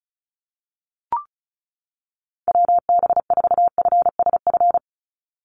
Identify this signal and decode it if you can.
Morse code, heard as a keyed tone, about 34 wpm, 700 Hz. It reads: WB4FSF
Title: Call Signs - 35wpm